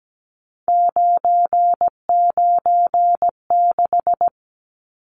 Morse code, heard as a keyed tone, about 17 wpm, 700 Hz. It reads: 996
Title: Numbers - 3-Digit 17wpm